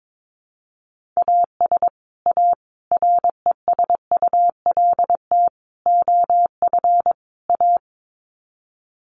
Morse code, heard as a keyed tone, about 22 wpm, 700 Hz. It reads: AS A RESULT OF A
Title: Sets of 5 Words Encore 22wpm